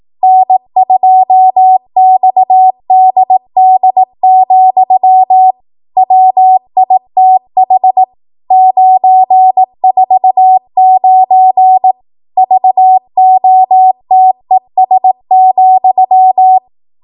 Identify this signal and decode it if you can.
Morse code, heard as a keyed tone, about 18 wpm, 750 Hz. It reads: N2XDD, WITH 949 VOTES,